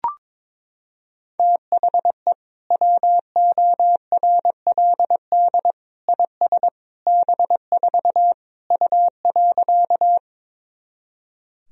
Morse code, heard as a keyed tone, about 22 wpm, 700 Hz. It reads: THE WORLD IS B4 U.
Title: Sentences from Top 200 Words and Abbreviations 22wpm